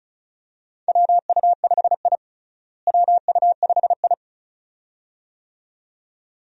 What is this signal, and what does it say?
Morse code, heard as a keyed tone, about 36 wpm, 700 Hz. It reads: WU5I WU5I
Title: USA Call Signs - 4 Characters - Repeated 2x - 35wpm